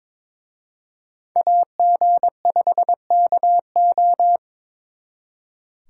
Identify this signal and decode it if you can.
Morse code, heard as a keyed tone, about 26 wpm, 700 Hz. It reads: AG5KO
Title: Speed Racing - USA 2x2 Call Signs 22wpm